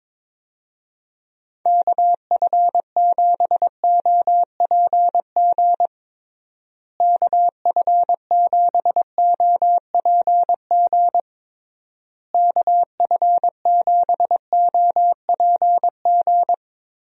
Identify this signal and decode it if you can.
Morse code, heard as a keyed tone, about 22 wpm, 700 Hz. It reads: KF7OPG KF7OPG KF7OPG
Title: USA Call Signs - 6 Characters - Repeated 3x - 22wpm